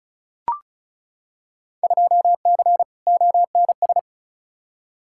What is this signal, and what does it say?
Morse code, heard as a keyed tone, about 36 wpm, 700 Hz. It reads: 2C ONS
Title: Field Day Exchanges Encore - 35wpm